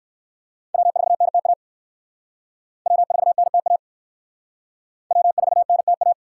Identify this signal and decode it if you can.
Morse code, heard as a keyed tone, about 60 wpm, 700 Hz. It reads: W4NTA W4NTA W4NTA